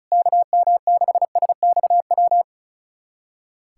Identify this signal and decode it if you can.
Morse code, heard as a keyed tone, about 36 wpm, 700 Hz. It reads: KM6SXW
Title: USA Callsign DE Callsign 35wpm